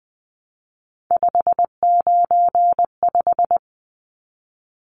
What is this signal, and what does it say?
Morse code, heard as a keyed tone, about 19 wpm, 700 Hz. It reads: 595
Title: Numbers - 3-Digit 20wpm